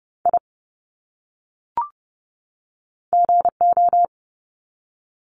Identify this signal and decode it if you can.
Morse code, heard as a keyed tone, about 30 wpm, 700 Hz. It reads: I GO
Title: Top 300 Words 30wpm